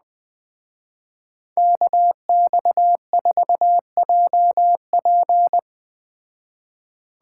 Morse code, heard as a keyed tone, about 20 wpm, 700 Hz. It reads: KX4JP
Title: Speed Racing - USA Call Signs 17wpm